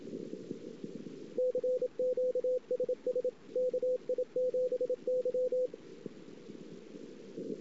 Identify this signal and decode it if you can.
Morse code, heard as a keyed tone, about 27 wpm, 500 Hz. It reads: CQSS KI7Y
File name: CW / 2008-ss-cw-k5tr-op-n5rz -19:00